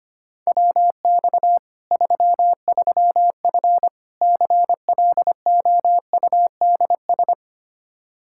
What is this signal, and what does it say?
Morse code, heard as a keyed tone, about 25 wpm, 700 Hz. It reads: WX 33F CLOUDS
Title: Practice QSO - Exchange 2 - Info - 25wpm